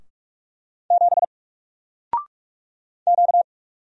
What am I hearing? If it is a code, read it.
Morse code, heard as a keyed tone, about 45 wpm, 700 Hz. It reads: 7 Q
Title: Single Letter-Number 45wpm